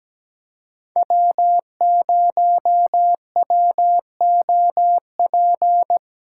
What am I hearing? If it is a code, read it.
Morse code, heard as a keyed tone, about 17 wpm, 700 Hz. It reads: W0WOP
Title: USA Callsign DE Callsign - 5 Characters - 17wpm